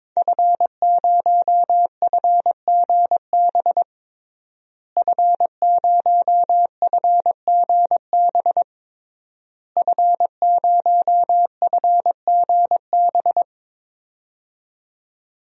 Morse code, heard as a keyed tone, about 22 wpm, 700 Hz. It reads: F0FGB F0FGB F0FGB
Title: DXCC-100 Call Signs - 5 Characters - Repeated 3x - 22wpm